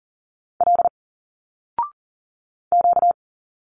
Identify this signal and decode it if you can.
Morse code, heard as a keyed tone, about 40 wpm, 700 Hz. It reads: L Q
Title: Single Letters 40wpm